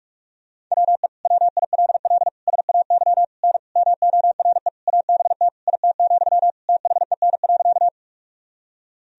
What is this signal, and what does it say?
Morse code, heard as a keyed tone, about 46 wpm, 700 Hz. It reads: WE WILL SAY N MORE ABT IT, THEN.
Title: Sentences from Top 200 Words and Abbreviations 45wpm